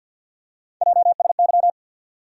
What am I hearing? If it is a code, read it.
Morse code, heard as a keyed tone, about 52 wpm, 700 Hz. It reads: JSY
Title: USA Call Sign Postfixes - 3 Letters 50wpm